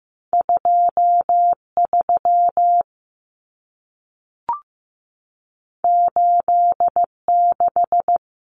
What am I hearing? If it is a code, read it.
Morse code, heard as a keyed tone, about 15 wpm, 700 Hz. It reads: 23 86
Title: Common Cut Numbers - 2-Digit 15wpm